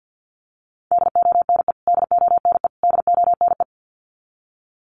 Morse code, heard as a keyed tone, about 50 wpm, 700 Hz. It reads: DONE DONE DONE
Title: Top 500 Words Repeated 50wpm